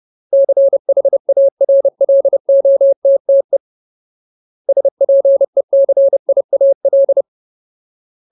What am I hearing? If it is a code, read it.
Morse code, heard as a keyed tone, about 30 wpm, 550 Hz. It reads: CHARLOTTE SPECIAL
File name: Ultimate Word List / 30WPM Ultimate